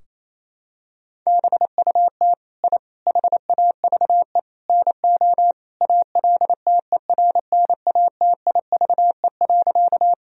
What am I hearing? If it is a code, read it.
Morse code, heard as a keyed tone, about 28 wpm, 700 Hz. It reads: BUT I HAVE NO ALTERNATIVE.
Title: Sentences from Top 1000 Words - Review Part 4 - 28wpm